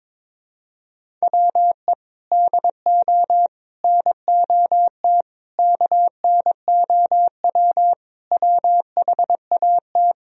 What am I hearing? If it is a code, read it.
Morse code, heard as a keyed tone, about 23 wpm, 700 Hz. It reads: WE DO NOT KNOW WHAT
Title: Speed-Racing - Sets of 5 Words 22wpm